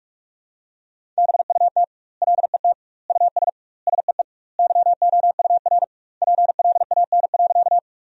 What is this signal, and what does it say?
Morse code, heard as a keyed tone, about 46 wpm, 700 Hz. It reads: BUT LET US SEE YOUR PLAN.
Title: 2x Word-Spacing - Sentences from Top 500 Words - 45wpm